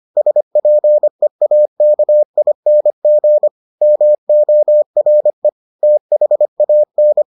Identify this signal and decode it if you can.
Morse code, heard as a keyed tone, about 25 wpm, 600 Hz. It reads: SPEAKING MORE THAN